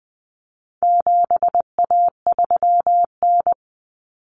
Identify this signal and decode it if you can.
Morse code, heard as a keyed tone, about 20 wpm, 700 Hz. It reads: 7A3N
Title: Common Cut Numbers - 4-Digit 20wpm